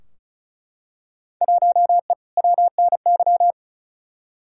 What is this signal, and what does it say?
Morse code, heard as a keyed tone, about 36 wpm, 700 Hz. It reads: 1E WNY
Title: Field Day Exchanges - 35wpm